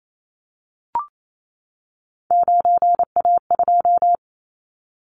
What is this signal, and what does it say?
Morse code, heard as a keyed tone, about 28 wpm, 700 Hz. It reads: 9A2
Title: Callsign Prefixes for Top 50 DXCC - 28wpm